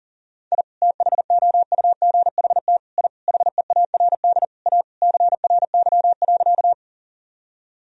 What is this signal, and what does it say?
Morse code, heard as a keyed tone, about 40 wpm, 700 Hz. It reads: I THOUGHT I HEARD A CRY.
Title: Sentences from Top 1000 Words - Review Part 2 - 40wpm